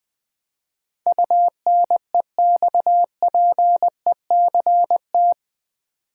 Morse code, heard as a keyed tone, about 20 wpm, 700 Hz. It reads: UNEXPECT
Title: Build A Word - 4 Letters At A Time - 10-Letter Words - 20wpm